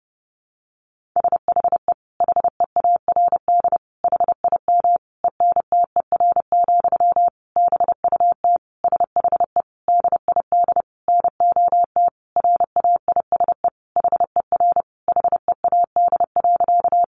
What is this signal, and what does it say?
Morse code, heard as a keyed tone, about 30 wpm, 700 Hz. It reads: SHE HEARD HIM ENTER, BUT SHE DID NOT RAISE HER HEAD.
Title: Sentences from Top 1400 Words 30wpm